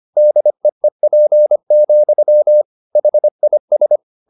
Morse code, heard as a keyed tone, about 25 wpm, 600 Hz. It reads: DEEP, HIS
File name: the old man and the sea part 1 25wpm